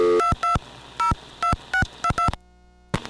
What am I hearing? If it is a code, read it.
DTMF, heard as telephone touch tones, tones of about 110 ms. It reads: 2202622